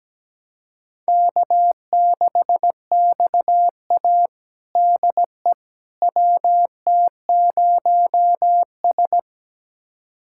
Morse code, heard as a keyed tone, about 17 wpm, 700 Hz. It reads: K6XA DE WT0S
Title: USA Callsign DE Callsign - 4 Characters - 17wpm